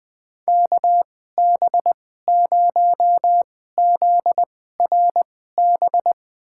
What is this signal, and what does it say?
Morse code, heard as a keyed tone, about 21 wpm, spaced as for 15 wpm, 700 Hz. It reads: KB0ZRB
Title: USA Call Signs - 6 Characters - 2x Character Spacing - 20wpm